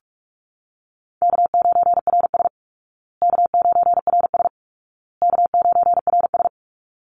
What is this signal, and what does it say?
Morse code, heard as a keyed tone, about 44 wpm, 700 Hz. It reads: K9RS K9RS K9RS
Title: Call Signs Repeated 2x - 45wpm